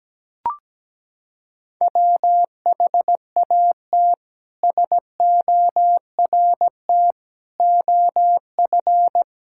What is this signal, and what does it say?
Morse code, heard as a keyed tone, about 17 wpm, 700 Hz. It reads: WHAT SORT OF
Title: Sentences from Top 1600 Words 17wpm